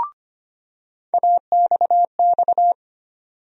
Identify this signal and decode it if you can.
Morse code, heard as a keyed tone, about 25 wpm, 700 Hz. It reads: AXX
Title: USA Call Sign Postfixes - 3 Letters 25wpm